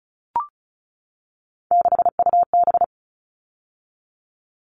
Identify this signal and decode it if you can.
Morse code, heard as a keyed tone, about 34 wpm, 700 Hz. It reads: 6UB
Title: All Cut Numbers - 3-Digit 35wpm